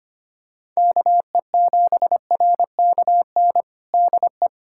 Morse code, heard as a keyed tone, about 25 wpm, 700 Hz. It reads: KE7RKN DE
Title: USA Callsign DE Callsign 25wpm